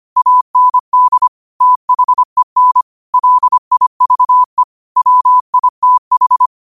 Morse code, heard as a keyed tone, about 25 wpm, 1000 Hz. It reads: AND THEN LIVE WITH
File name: qotdfiles / 25wpm